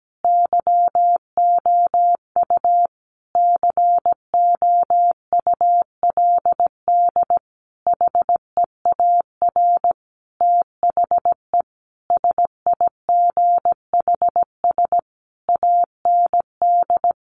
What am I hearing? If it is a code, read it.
Morse code, heard as a keyed tone, about 17 wpm, 700 Hz. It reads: YOU COULD HEAR THE SIGHS AND
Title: A Sound Of Thunder 17wpm